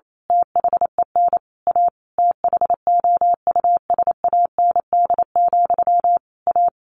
Morse code, heard as a keyed tone, about 28 wpm, 700 Hz. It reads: THEN A THOUSAND, A
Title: A Sound Of Thunder 28wpm